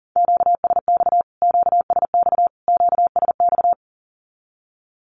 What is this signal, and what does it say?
Morse code, heard as a keyed tone, about 40 wpm, 700 Hz. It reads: QSX QSX QSX
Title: Q-Codes Repeated 3x 40wpm